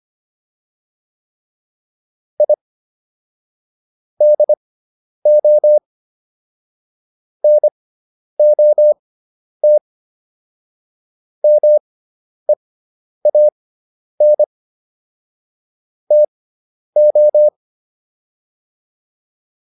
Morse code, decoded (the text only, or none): I DO NOT MEAN TO